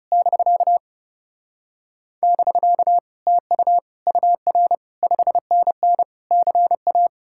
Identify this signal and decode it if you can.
Morse code, heard as a keyed tone, about 40 wpm, spaced as for 35 wpm, 700 Hz. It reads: <BK> <BK> TU UR 5NN CA
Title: Speed-Racing Common QSO Phrases Encore - 30wpm